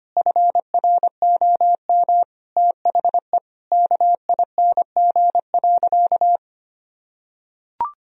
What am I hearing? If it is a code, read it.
Morse code, heard as a keyed tone, about 25 wpm, 700 Hz. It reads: FROM THE KING.
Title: Sentences from Top 1000 Words - Review Part 2 - 25wpm